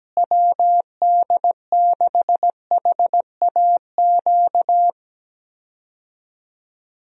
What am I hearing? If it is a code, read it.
Morse code, heard as a keyed tone, about 17 wpm, 700 Hz. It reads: WD6HAQ
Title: USA Call Signs - 6 Characters - Repeated 2x - 17wpm